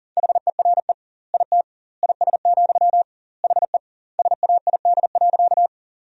Morse code, heard as a keyed tone, about 40 wpm, 700 Hz. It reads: HERE IT IS, HE SAID.
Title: 2x Word-Spacing - Sentences from Top 200 Words - 40wpm